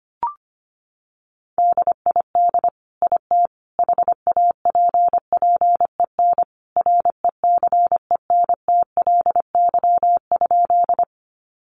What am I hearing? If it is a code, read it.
Morse code, heard as a keyed tone, about 25 wpm, 700 Hz. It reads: DID IT HAPPEN RECENTLY?
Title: Sentences from Top 1200 Words 25wpm